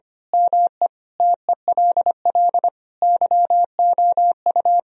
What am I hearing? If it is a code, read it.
Morse code, heard as a keyed tone, about 25 wpm, 700 Hz. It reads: ME TELL YOU
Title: Speed-Racing - Sets of 5 Words 25wpm